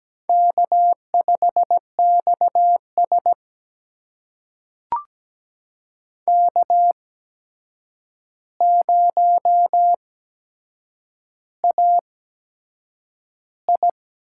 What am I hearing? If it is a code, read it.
Morse code, heard as a keyed tone, about 17 wpm, spaced as for 7 wpm, 700 Hz. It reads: K5XS K0AI
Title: USA Call Signs - 4 Characters - 8x Character Spacing - 17wpm